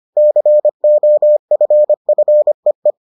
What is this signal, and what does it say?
Morse code, heard as a keyed tone, about 25 wpm, 600 Hz. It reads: COFFEE